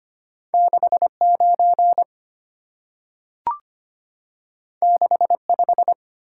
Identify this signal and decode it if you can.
Morse code, heard as a keyed tone, about 25 wpm, 700 Hz. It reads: 69 65
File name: Two Letter-Number Combinations - 25wpm